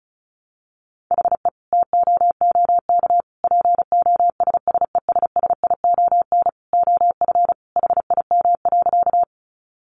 Morse code, heard as a keyed tone, about 34 wpm, 700 Hz. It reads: HE TOOK POSSESSION OF HIM.
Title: Sentences from Top 1400 Words 35wpm